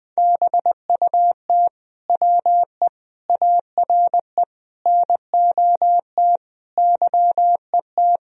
Morse code, heard as a keyed tone, about 20 wpm, 700 Hz. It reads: BUT WE ARE NOT YET
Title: Sentences from Top 1000 Words - Review Part 1 - 20wpm